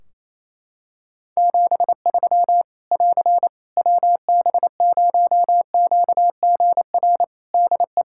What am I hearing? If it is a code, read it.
Morse code, heard as a keyed tone, about 28 wpm, 700 Hz. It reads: 73 <AR> WB0QGR DE
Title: Practice QSO - Exchange 4 - Ending - 28wpm